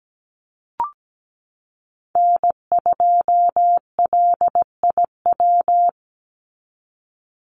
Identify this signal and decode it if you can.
Morse code, heard as a keyed tone, about 17 wpm, 700 Hz. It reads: N2LIW
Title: USA Call Signs 17wpm